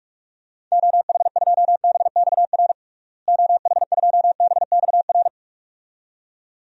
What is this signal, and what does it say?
Morse code, heard as a keyed tone, about 46 wpm, 700 Hz. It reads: OH2BXR OH2BXR
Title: DXCC-100 Call Signs - 6 Characters - Repeated 2x - 45wpm